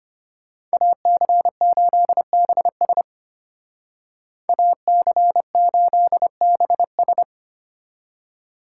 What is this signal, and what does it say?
Morse code, heard as a keyed tone, about 33 wpm, 700 Hz. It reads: AC8BS AC8BS